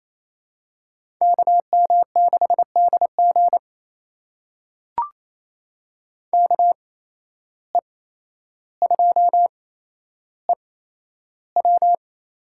Morse code, heard as a keyed tone, about 28 wpm, spaced as for 12 wpm, 700 Hz. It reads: KM6DG KE2EW
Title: USA Call Signs - 5 Characters - 8x Character Spacing - 28wpm